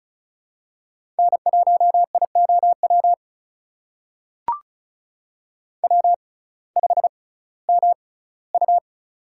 Morse code, heard as a keyed tone, about 36 wpm, spaced as for 19 wpm, 700 Hz. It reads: N1IOW W5MU
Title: USA Call Signs - 6x Character Spacing - 35wpm